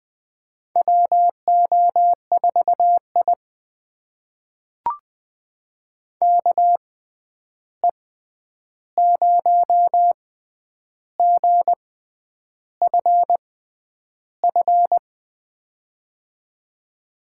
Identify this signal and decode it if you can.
Morse code, heard as a keyed tone, about 20 wpm, spaced as for 9 wpm, 700 Hz. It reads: WO4I KE0GFF
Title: USA Call Signs - 6x Character Spacing - 20wpm